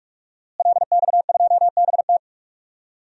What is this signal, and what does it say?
Morse code, heard as a keyed tone, about 48 wpm, 700 Hz. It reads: LX2BT
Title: DXCC-100 Call Signs - 5 Characters - Repeated 2x - 45wpm